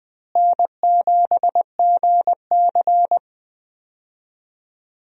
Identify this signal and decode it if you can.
Morse code, heard as a keyed tone, about 20 wpm, 700 Hz. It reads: N7GC
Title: USA Call Signs - 4 Characters - Repeated 2x - 20wpm